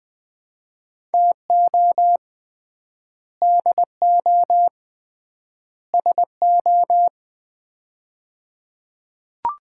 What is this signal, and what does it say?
Morse code, heard as a keyed tone, about 20 wpm, 700 Hz. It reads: TO DO SO
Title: 3x Word-Spacing - Sets of 3 Words - 20wpm